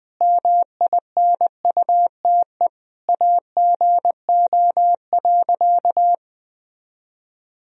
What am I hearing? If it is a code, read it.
Morse code, heard as a keyed tone, about 20 wpm, 700 Hz. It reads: MINUTE AGO.
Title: Sentences from Top 500 Words and Abbreviations 20wpm